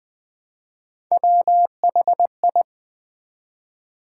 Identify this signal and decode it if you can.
Morse code, heard as a keyed tone, about 20 wpm, 700 Hz. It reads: WHI
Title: Build A Word - 1 Letter At A Time - 5-Letter Words - 20wpm